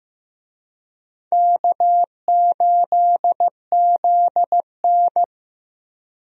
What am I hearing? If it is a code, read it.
Morse code, heard as a keyed tone, about 15 wpm, 700 Hz. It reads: K8ZN
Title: USA Call Signs - 4 Characters - Repeated 2x - 15wpm